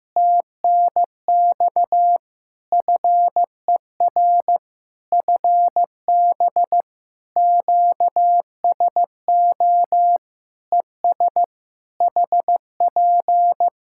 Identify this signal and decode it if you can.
Morse code, heard as a keyed tone, about 15 wpm, 700 Hz. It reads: TNX FER FB QSO ES HP
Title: Practice QSO - Exchange 4 - Ending - 15wpm